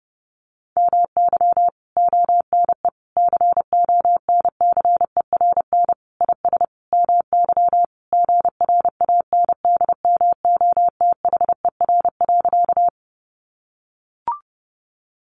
Morse code, heard as a keyed tone, about 30 wpm, 700 Hz. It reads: MY ONE CONCERN IS MY GRANDMOTHER.
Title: Sentences from Top 1200 Words 30wpm